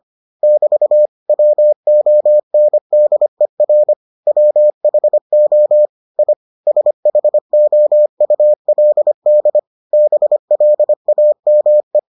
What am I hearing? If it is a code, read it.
Morse code, heard as a keyed tone, about 25 wpm, 600 Hz. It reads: <BT> WONDER WHO I SHOULD BLAME